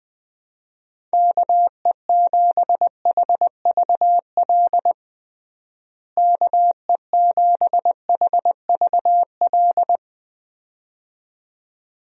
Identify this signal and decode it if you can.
Morse code, heard as a keyed tone, about 20 wpm, 700 Hz. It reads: KE7HVL KE7HVL
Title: USA Call Signs - 6 Characters - Repeated 2x - 20wpm